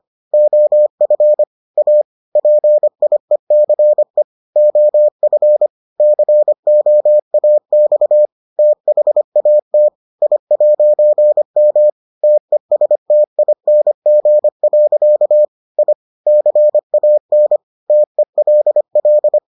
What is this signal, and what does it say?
Morse code, heard as a keyed tone, about 25 wpm, 600 Hz. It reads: OF A PIECE OF COAX THAT I'M TESTING. I CAN TELL